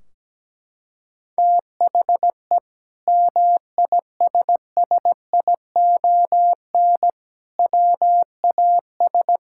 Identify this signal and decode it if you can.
Morse code, heard as a keyed tone, about 17 wpm, 700 Hz. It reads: THE MISSION WAS
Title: Sentences from Top 1600 Words 17wpm